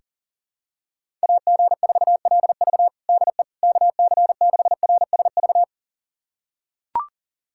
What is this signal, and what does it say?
Morse code, heard as a keyed tone, about 40 wpm, 700 Hz. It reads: AG4LV DE KC6RSV
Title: USA Callsign DE Callsign 40wpm